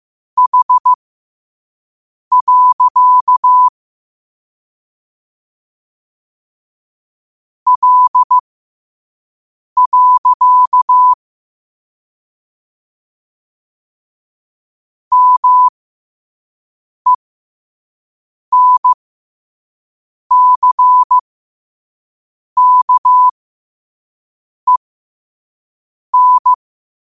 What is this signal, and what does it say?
Morse code, heard as a keyed tone, about 15 wpm, spaced as for 5 wpm, 1000 Hz. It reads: H. L. MENCKEN
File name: qotdfiles / 5wpm